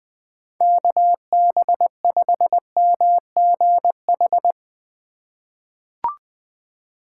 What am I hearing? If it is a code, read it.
Morse code, heard as a keyed tone, about 20 wpm, 700 Hz. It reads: KB5MGH
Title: USA Call Signs - 6 Characters - 2x Character Spacing - 20wpm